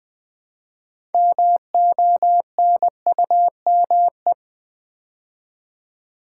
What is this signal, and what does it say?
Morse code, heard as a keyed tone, about 20 wpm, 700 Hz. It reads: MONUME
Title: Build A Word - 2 Letters At A Time - 8-Letter Words - 20wpm